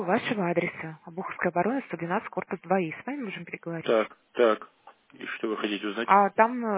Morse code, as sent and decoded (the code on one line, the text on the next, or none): none